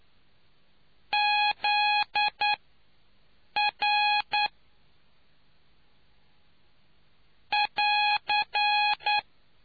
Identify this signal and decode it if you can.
Morse code, heard as a keyed tone, about 9 wpm, spaced as for 6 wpm, 830 Hz. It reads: ZR <AR>